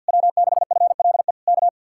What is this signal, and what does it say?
Morse code, heard as a keyed tone, about 66 wpm, spaced as for 57 wpm, 700 Hz. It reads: W6FLE K
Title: Speed-Racing Common QSO Phrases Encore - 50wpm